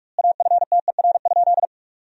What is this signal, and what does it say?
Morse code, heard as a keyed tone, about 46 wpm, 700 Hz. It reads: AFTER?
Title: 2x Word-Spacing - Sentences from Top 200 Words - 45wpm